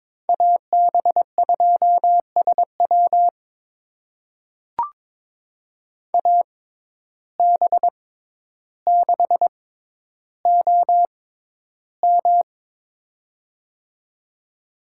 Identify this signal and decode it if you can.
Morse code, heard as a keyed tone, about 22 wpm, spaced as for 11 wpm, 700 Hz. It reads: AB2SW AB6OM
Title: USA Call Signs - 5 Characters - 6x Character Spacing - 22wpm